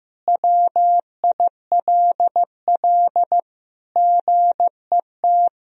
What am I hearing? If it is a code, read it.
Morse code, heard as a keyed tone, about 15 wpm, 700 Hz. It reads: WILL GET
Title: Sentences from Top 1000 Words - Review Part 1 - 15wpm